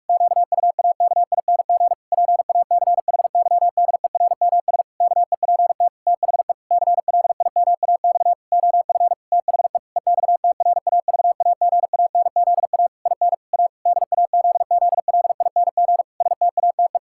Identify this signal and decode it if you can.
Morse code, heard as a keyed tone, about 46 wpm, 700 Hz. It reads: QUAKING PACHYDERMS KEPT THE CLIMAX OF THE EXTRAVAGANZA IN A DAZZLING STATE
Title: Pangram Challenge v2 - 45wpm